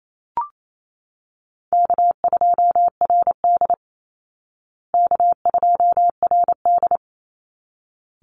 Morse code, heard as a keyed tone, about 28 wpm, 700 Hz. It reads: K2RD K2RD
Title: Easy Call Signs Repeated 2x - 28wpm